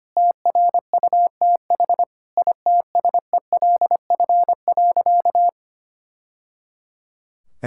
Morse code, heard as a keyed tone, about 25 wpm, 700 Hz. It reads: TRUTH ITSELF.
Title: Sentences from Top 1000 Words - Review Part 5 - 25wpm